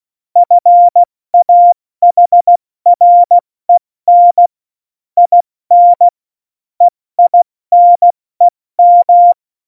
Morse code, heard as a keyed tone, about 16 wpm, 700 Hz. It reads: FAHREN IN EINEM
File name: Morsekurs / Witze / Witz 07. November 2017, 70 F80